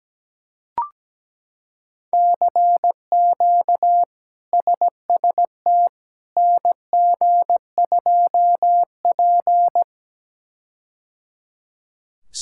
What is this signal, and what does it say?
Morse code, heard as a keyed tone, about 17 wpm, 700 Hz. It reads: CQ SST NG2P
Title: K1USN SST QSO Practice - 17wpm